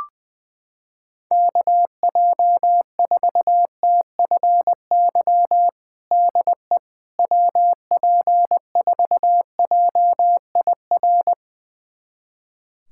Morse code, heard as a keyed tone, about 20 wpm, 700 Hz. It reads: KJ4TFY DE WP4JIR
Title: USA Callsign DE Callsign 20wpm